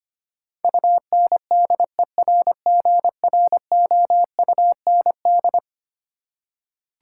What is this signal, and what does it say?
Morse code, heard as a keyed tone, about 25 wpm, 700 Hz. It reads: UNDERGROUND
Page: morse-code-ninja-practice-25wpm